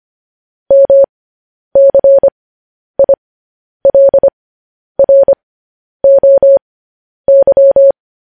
Morse code, heard as a keyed tone, about 25 wpm, spaced as for 10 wpm, 550 Hz. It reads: MCILROY
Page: News Headlines in Morse Code at 10 WPM